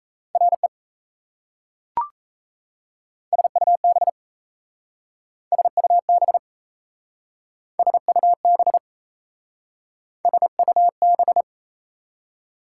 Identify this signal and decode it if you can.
Morse code, heard as a keyed tone, about 37 wpm, 690 Hz. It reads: RE SUB SUB SUB SUB